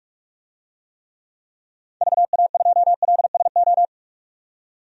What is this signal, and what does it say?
Morse code, heard as a keyed tone, about 46 wpm, 700 Hz. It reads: VA2LSO
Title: DXCC-100 Call Signs - 6 Characters - 45wpm